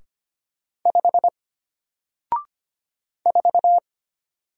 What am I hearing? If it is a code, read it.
Morse code, heard as a keyed tone, about 25 wpm, 700 Hz. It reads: 5 4
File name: Single-Letter-Number-25wpm